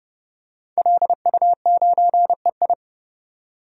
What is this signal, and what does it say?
Morse code, heard as a keyed tone, about 30 wpm, 700 Hz. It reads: LU9EI
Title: DXCC-100 Call Signs - 5 Characters - 30wpm